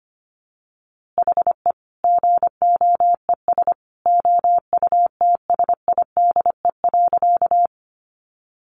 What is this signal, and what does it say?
Morse code, heard as a keyed tone, about 25 wpm, 700 Hz. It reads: HE GOES OUTSIDE.
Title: Easy Phrases 25wpm